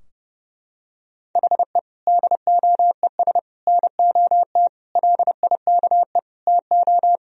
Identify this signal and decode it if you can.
Morse code, heard as a keyed tone, about 30 wpm, 700 Hz. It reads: HE DOES NOT LIKE TO
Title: Sentences from Top 1000 Words - Review Part 3 - 30wpm